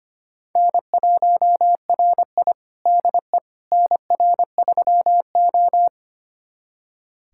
Morse code, heard as a keyed tone, about 25 wpm, 700 Hz. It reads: N1RI DE NR3O
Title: USA Callsign DE Callsign - 4 Characters - 25wpm